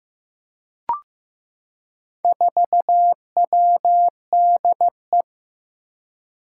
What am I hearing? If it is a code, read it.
Morse code, heard as a keyed tone, about 15 wpm, 700 Hz. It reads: 4WDE
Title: Four Challenging Letter-Number Combinations 15wpm